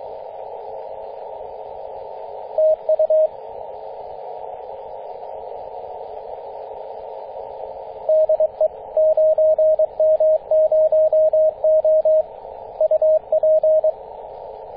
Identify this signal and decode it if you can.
Morse code, heard as a keyed tone, about 22 wpm, 630 Hz, with radio noise. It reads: TU DE 9M0O UP
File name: 9m0o-24-891mhz-cw